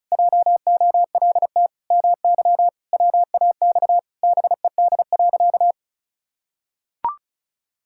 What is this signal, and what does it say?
Morse code, heard as a keyed tone, about 36 wpm, 700 Hz. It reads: JOLT MY WAX BED.
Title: Pangram Challenge v2 - 35wpm